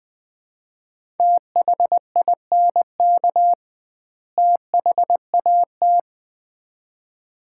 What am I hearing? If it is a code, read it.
Morse code, heard as a keyed tone, about 21 wpm, 700 Hz. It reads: THINK THAT
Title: 2x Word-Spacing - Sets of 2 Words - 20wpm